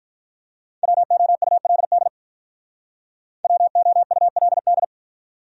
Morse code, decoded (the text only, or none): WOULD WOULD